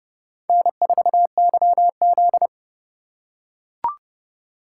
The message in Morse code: N4YZ

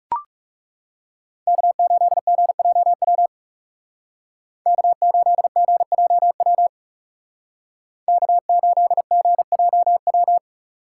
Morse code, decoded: K8GJW K8GJW K8GJW

